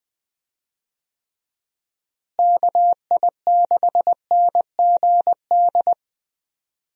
Morse code, heard as a keyed tone, about 20 wpm, 700 Hz. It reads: KI6NGD
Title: USA Call Signs - 6 Characters - Repeated 3x - 20wpm